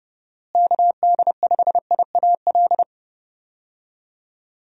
Morse code, heard as a keyed tone, about 30 wpm, 700 Hz. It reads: KD5IAL
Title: USA Call Signs - 6 Characters - Repeated 2x - 30wpm